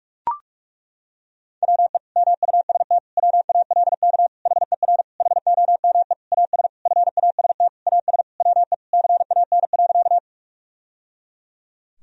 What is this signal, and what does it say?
Morse code, heard as a keyed tone, about 41 wpm, 700 Hz. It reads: WE MUST WALK HER HOME AS FAST AS WE CAN.
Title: Sentences from Top 1000 Words - Review Part 2 - 45wpm